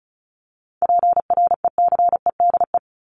Morse code, heard as a keyed tone, about 34 wpm, 700 Hz. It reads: PRECEDE
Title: ING Suffix - Top 100 Words with Base Words 35wpm